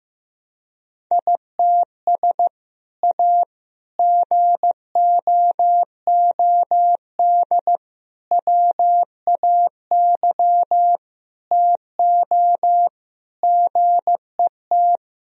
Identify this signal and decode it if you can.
Morse code, heard as a keyed tone, about 15 wpm, 700 Hz. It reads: ITS A GOOD WAY TO GET
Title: Sentences from Top 1000 Words - Review Part 3 - 15wpm